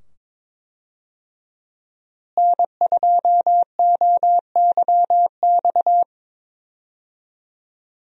Morse code, heard as a keyed tone, about 22 wpm, 700 Hz. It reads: N2OYX
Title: USA Call Signs - 5 Characters - Repeated 2x - 22wpm